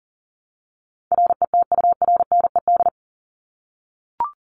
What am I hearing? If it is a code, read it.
Morse code, heard as a keyed tone, about 40 wpm, 700 Hz. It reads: RETURNED
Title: ED Suffix - Top 100 Words with Base Words 40wpm